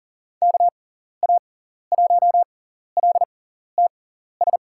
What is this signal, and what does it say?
Morse code, heard as a keyed tone, about 40 wpm, spaced as for 14 wpm, 700 Hz. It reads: KA1LTS
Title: USA Call Signs - 6x Character Spacing - 40wpm